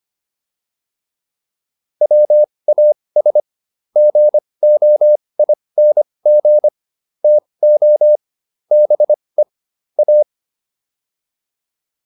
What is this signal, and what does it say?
Morse code, heard as a keyed tone, about 25 wpm, spaced as for 20 wpm, 600 Hz. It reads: WAS GOING TO BE A